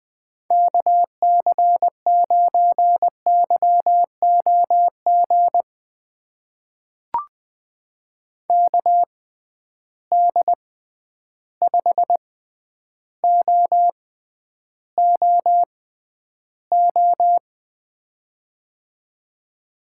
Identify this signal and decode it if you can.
Morse code, heard as a keyed tone, about 20 wpm, spaced as for 10 wpm, 700 Hz. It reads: KC9YOG KD5OOO
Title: USA Call Signs - 6 Characters - 6x Character Spacing - 20wpm